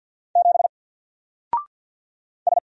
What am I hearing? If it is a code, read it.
Morse code, heard as a keyed tone, about 48 wpm, 700 Hz. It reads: 7 S
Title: Single Letter-Number Alt 50wpm